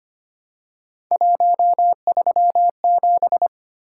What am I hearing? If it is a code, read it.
Morse code, heard as a keyed tone, about 25 wpm, 700 Hz. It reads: 137
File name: Three Letter-Number Combinations - 25wpm